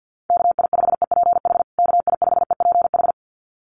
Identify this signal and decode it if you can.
Morse code, heard as a keyed tone, about 48 wpm, 700 Hz. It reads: KI5EPH KI5EPH
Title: USA Call Signs Repeated 2x 50wpm